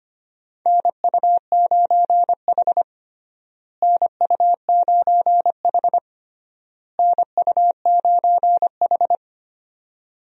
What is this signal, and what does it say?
Morse code, heard as a keyed tone, about 25 wpm, 700 Hz. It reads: NU9H NU9H NU9H
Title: USA Call Signs - 4 Characters - Repeated 3x - 25wpm